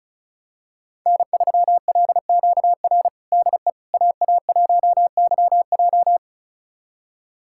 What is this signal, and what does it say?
Morse code, heard as a keyed tone, about 34 wpm, 700 Hz. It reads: N3LQR DE AA1YJ
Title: USA Callsign DE Callsign - 5 Characters - 35wpm